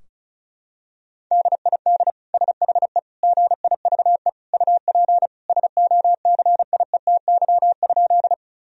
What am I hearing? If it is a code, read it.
Morse code, heard as a keyed tone, about 36 wpm, 700 Hz. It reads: DID SHE GIVE UP SOCIETY?
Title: Sentences from Top 1000 Words - Review Part 4 - 35wpm